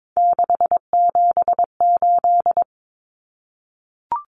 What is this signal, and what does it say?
Morse code, heard as a keyed tone, about 22 wpm, 700 Hz. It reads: 678
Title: Common Cut Numbers - 3-Digit 22wpm